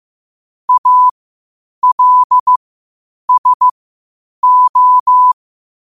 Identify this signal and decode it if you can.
Morse code, heard as a keyed tone, about 15 wpm, spaced as for 8 wpm, 1000 Hz. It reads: ALSO
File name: qotdfiles / 8wpm